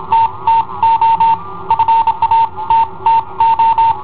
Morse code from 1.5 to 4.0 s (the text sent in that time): FA TTO